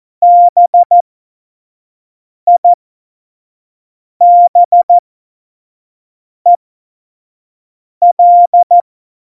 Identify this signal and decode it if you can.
Morse code, heard as a keyed tone, about 14 wpm, spaced as for 5 wpm, 700 Hz. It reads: BIBEL
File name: Morsekurs / Witze / Witz 15. Dezember, F70